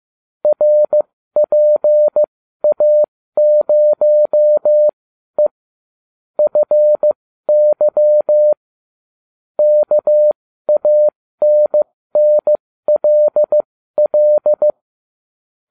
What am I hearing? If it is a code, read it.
Morse code, heard as a keyed tone, about 15 wpm, spaced as for 12 wpm, 600 Hz. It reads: RPA0E FY KANNLL